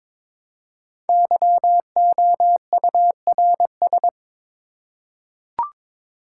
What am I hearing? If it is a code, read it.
Morse code, heard as a keyed tone, about 22 wpm, 700 Hz. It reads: YOURS
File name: 5 Letter Words - First 3 Letters Given - 22wpm